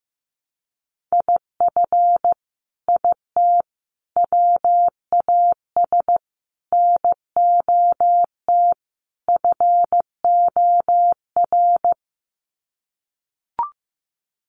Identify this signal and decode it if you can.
Morse code, heard as a keyed tone, about 15 wpm, 700 Hz. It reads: IF IT WAS NOT FOR
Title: Sets of 5 Words Encore 15wpm